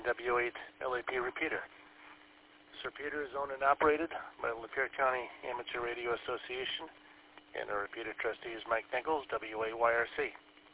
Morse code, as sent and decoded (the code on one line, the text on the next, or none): none